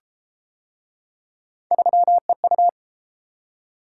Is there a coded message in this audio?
Yes: Morse code from 1.7 to 2.7 s, heard as a keyed tone, about 29 wpm, 700 Hz. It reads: 3EU